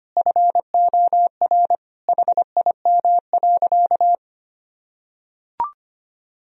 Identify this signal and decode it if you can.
Morse code, heard as a keyed tone, about 25 wpm, 700 Hz. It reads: FOR HIM.
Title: Sentences from Top 1000 Words - Review Part 1 - 25wpm